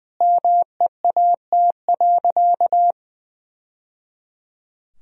Morse code, heard as a keyed tone, about 20 wpm, 700 Hz. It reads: MEAT.
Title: Sentences from Top 1000 Words - Review Part 3 - 20wpm